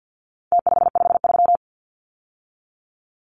Morse code, heard as a keyed tone, about 50 wpm, 700 Hz. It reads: T553
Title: Numbers - Contest Style - 4-Digit 50wpm